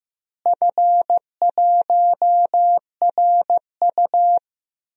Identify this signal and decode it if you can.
Morse code, heard as a keyed tone, about 15 wpm, 700 Hz. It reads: F1RU
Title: DXCC-100 Call Signs - 4 Characters - Repeated 3x - 15wpm